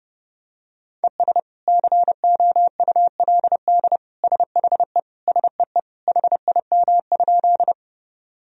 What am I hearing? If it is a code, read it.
Morse code, heard as a keyed tone, about 30 wpm, 700 Hz. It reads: ES COULD SHE SEE HIM?
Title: Sentences from Top 100 Words and Abbreviations 30wpm